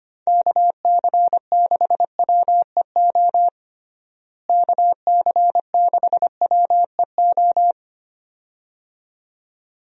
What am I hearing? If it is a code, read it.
Morse code, heard as a keyed tone, about 25 wpm, 700 Hz. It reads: KC6WEO KC6WEO
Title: USA Call Signs - 6 Characters - Repeated 2x - 25wpm